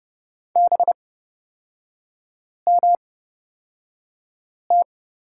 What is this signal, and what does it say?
Morse code, heard as a keyed tone, about 30 wpm, spaced as for 5 wpm, 700 Hz. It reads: BMT
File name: Lesson 37e - Letter x - Three Characters 30wpm